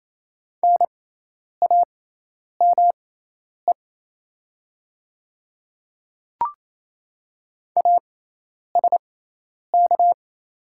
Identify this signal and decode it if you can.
Morse code, heard as a keyed tone, about 28 wpm, spaced as for 10 wpm, 690 Hz. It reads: NAME ASK
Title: 6x Character Spacing - Top 300 Words - 28wpm